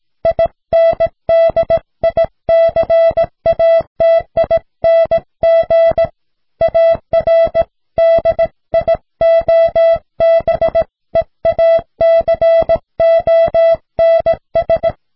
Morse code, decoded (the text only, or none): INDICATING ARDIOBEACONS